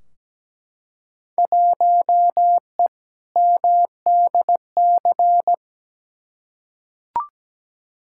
Morse code, heard as a keyed tone, about 17 wpm, 700 Hz. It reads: 1E MDC
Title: Field Day Exchanges Encore - 17wpm